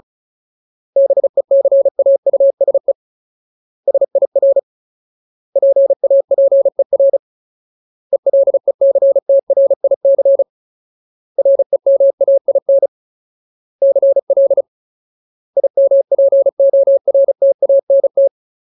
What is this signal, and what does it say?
Morse code, heard as a keyed tone, about 34 wpm, 550 Hz. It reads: BECAUSE SIR PAPER ELECTRIC REMAIN CL IMPORTANT
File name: Ultimate Word List / 35WPM Ultimate